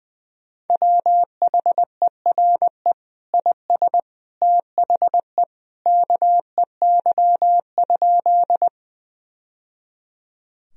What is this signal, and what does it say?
Morse code, heard as a keyed tone, about 20 wpm, 700 Hz. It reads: WHERE IS THE KEY?
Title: Sentences from Top 1000 Words - Review Part 4 - 20wpm